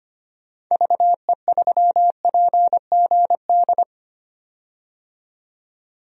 Morse code, heard as a keyed tone, about 25 wpm, 700 Hz. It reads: VE3PGD
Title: DXCC-100 Call Signs - 6 Characters - 25wpm